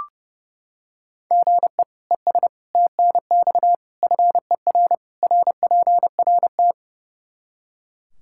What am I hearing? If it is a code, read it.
Morse code, heard as a keyed tone, about 32 wpm, 700 Hz. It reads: GE ES TNX FER RPRT
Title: Practice QSO - Exchange 1 - Beginning - 30wpm